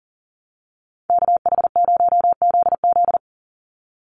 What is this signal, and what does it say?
Morse code, heard as a keyed tone, about 38 wpm, 700 Hz. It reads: KH0ZZ
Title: Call Signs - 40wpm